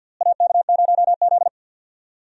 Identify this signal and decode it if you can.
Morse code, heard as a keyed tone, about 47 wpm, 700 Hz. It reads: AK0Z
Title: USA Call Signs - 4 Characters - Repeated 2x - 50wpm